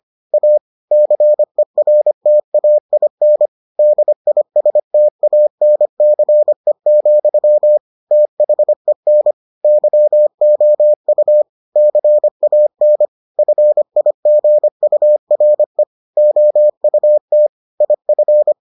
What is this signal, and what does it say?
Morse code, heard as a keyed tone, about 25 wpm, 600 Hz. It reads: A CERTAIN DISTANCE, THEN YOU CAN FIGURE OUT IF